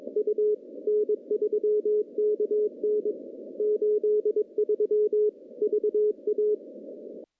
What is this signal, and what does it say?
Morse code, heard as a keyed tone, about 23 wpm, 420 Hz. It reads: U N3KN 83 VA
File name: contests / CW-SS-k5tr- -1400